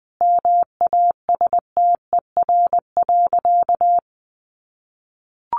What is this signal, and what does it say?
Morse code, heard as a keyed tone, about 20 wpm, 700 Hz. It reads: MASTER.
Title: Sentences from Top 1300 Words 20wpm